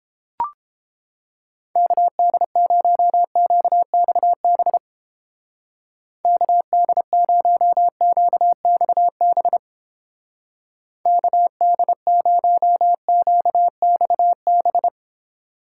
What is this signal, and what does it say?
Morse code, heard as a keyed tone, about 29 wpm, 700 Hz. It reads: KD0QXB KD0QXB KD0QXB